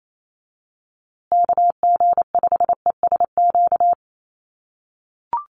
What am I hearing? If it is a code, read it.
Morse code, heard as a keyed tone, about 28 wpm, 700 Hz. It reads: KG5ESQ
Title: USA Call Signs Repeated 3x 28wpm